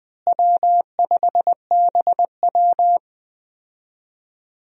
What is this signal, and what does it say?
Morse code, heard as a keyed tone, about 20 wpm, 700 Hz. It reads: W5BW
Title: USA Call Signs - 4 Characters - Repeated 2x - 20wpm